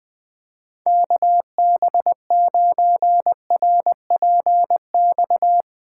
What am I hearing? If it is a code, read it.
Morse code, heard as a keyed tone, about 20 wpm, 700 Hz. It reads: KB9RPX
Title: USA Callsign DE Callsign - 6 Characters - 20wpm